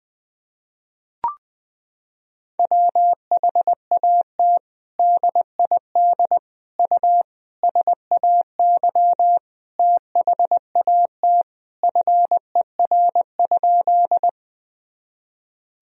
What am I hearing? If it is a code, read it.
Morse code, heard as a keyed tone, about 20 wpm, 700 Hz. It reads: WHAT DID U SAY THAT FER?
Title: Sentences from Top 200 Words and Abbreviations 20wpm